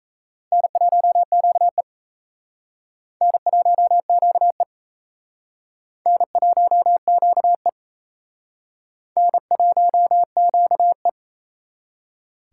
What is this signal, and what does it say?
Morse code, heard as a keyed tone, about 36 wpm, 700 Hz. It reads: N1QE N1QE N1QE N1QE